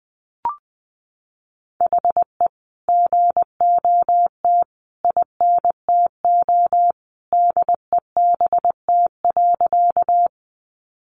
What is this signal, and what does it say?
Morse code, heard as a keyed tone, about 19 wpm, 700 Hz. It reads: HE GOT INTO DEBT.
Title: Sentences from Top 1300 Words 20wpm